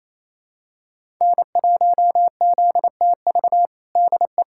Morse code, heard as a keyed tone, about 28 wpm, 700 Hz. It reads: N1ZTV DE
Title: USA Callsign DE Callsign 28wpm